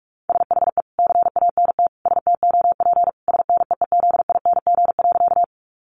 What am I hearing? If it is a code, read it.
Morse code, heard as a keyed tone, about 44 wpm, 700 Hz. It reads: SHE CANT STOP SNEEZING.
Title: Easy Phrases 45wpm